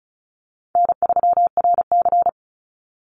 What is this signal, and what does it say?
Morse code, heard as a keyed tone, about 34 wpm, 700 Hz. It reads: N3RC
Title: Call Signs Repeated 3x - 35wpm